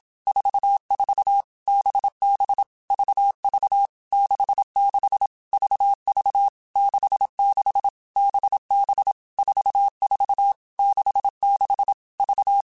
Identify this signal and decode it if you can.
Morse code, heard as a keyed tone, about 27 wpm, 800 Hz. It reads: 44 BB VV 66 VV 66 BB 44 66 V
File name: Basic Letter Groups By7Groups / VB46 / VB46@25